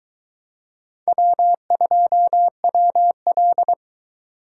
Morse code, heard as a keyed tone, about 22 wpm, 700 Hz. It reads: W2WL